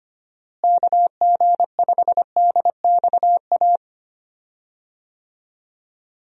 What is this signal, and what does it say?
Morse code, heard as a keyed tone, about 25 wpm, 700 Hz. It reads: KG5DXA
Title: USA Call Signs - 6 Characters - Repeated 2x - 25wpm